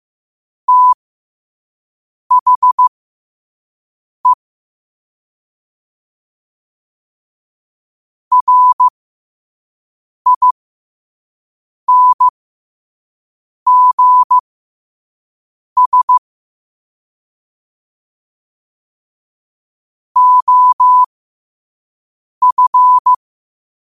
Morse code, heard as a keyed tone, about 15 wpm, spaced as for 5 wpm, 1000 Hz. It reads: THE RINGS OF
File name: qotdfiles / 5wpm